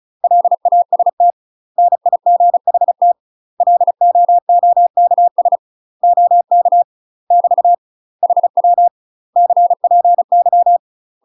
Morse code, heard as a keyed tone, about 36 wpm, 700 Hz. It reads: LAST NIGHT LOOKS OK <BT> HW CPY
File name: UK MASTER QSOs TRAINING / 35WPM / UK MASTER QSOs 0028